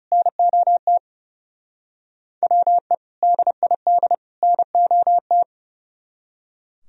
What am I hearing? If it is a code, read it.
Morse code, heard as a keyed tone, about 36 wpm, 700 Hz. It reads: NOT WE DID NOT
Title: Speed-Racing - Sets of 3 Words 30wpm